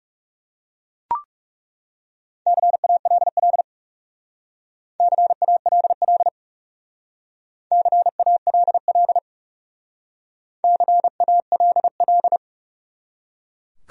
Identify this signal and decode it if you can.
Morse code, heard as a keyed tone, about 37 wpm, 700 Hz. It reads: CALL CALL CALL CALL
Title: Speed-Racing Top 500 Words 30wpm